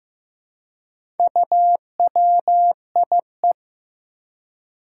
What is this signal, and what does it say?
Morse code, heard as a keyed tone, about 15 wpm, 700 Hz. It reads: UWIE
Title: Four Challenging Letter-Number Combinations 15wpm